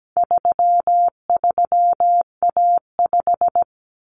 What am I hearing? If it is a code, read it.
Morse code, heard as a keyed tone, about 17 wpm, 700 Hz. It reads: 33A5
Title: Common Cut Numbers - 4-Digit 17wpm